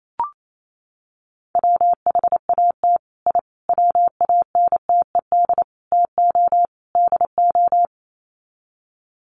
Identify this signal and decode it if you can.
Morse code, heard as a keyed tone, about 28 wpm, 700 Hz. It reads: WHAT I WANTED TO DO
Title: Sets of 5 Words Encore2 28wpm